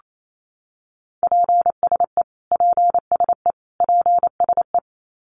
Morse code, heard as a keyed tone, about 28 wpm, 700 Hz. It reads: PSE PSE PSE
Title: Abbreviations Repeated 3x - 28wpm